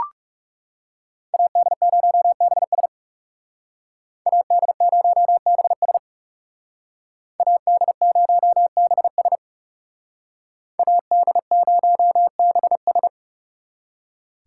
Morse code, heard as a keyed tone, about 36 wpm, 700 Hz. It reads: AD0BS AD0BS AD0BS AD0BS
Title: Speed Racing - USA Call Signs 30wpm